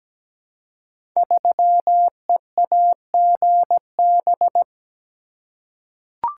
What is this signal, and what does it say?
Morse code, heard as a keyed tone, about 17 wpm, 700 Hz. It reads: 3EAGB
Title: Code Groups - 5 Letter:Numbers - 17wpm